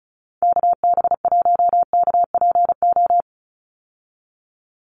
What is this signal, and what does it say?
Morse code, heard as a keyed tone, about 34 wpm, 700 Hz. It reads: KB1KPO
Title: USA Call Signs 35wpm